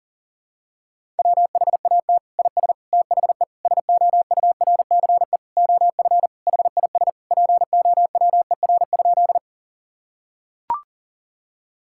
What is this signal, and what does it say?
Morse code, heard as a keyed tone, about 40 wpm, 700 Hz. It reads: WHAT IS THE SOURCE OF HIS POWER?
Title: Sentences from Top 1000 Words - Review Part 5 - 40wpm